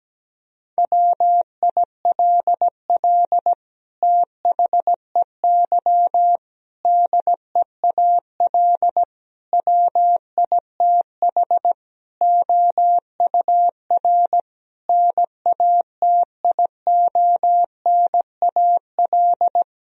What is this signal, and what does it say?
Morse code, heard as a keyed tone, about 17 wpm, 700 Hz. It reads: WILL THEY DEAL WITH OUR NATIONAL
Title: Sentences from Top 1000 Words - Review Part 4 - 17wpm